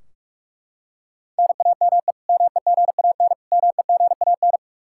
Morse code, heard as a keyed tone, about 43 wpm, 700 Hz. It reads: NAME MEGAN MEGAN
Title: Practice QSO - Exchange 1 - Beginning - 45wpm